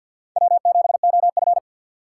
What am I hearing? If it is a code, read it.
Morse code, heard as a keyed tone, about 52 wpm, 700 Hz. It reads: W7OF
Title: USA Call Signs - 2x Character Spacing - 50wpm